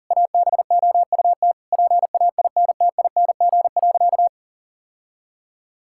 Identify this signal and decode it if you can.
Morse code, heard as a keyed tone, about 40 wpm, 700 Hz. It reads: ABOUT PAINTING.
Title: Sentences from Top 1000 Words - Review Part 6 - 40wpm